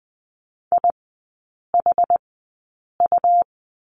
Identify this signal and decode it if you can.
Morse code, heard as a keyed tone, about 20 wpm, spaced as for 8 wpm, 690 Hz. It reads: IHU
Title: 191 Lesson 31e - Letter v - Three Characters 20wpm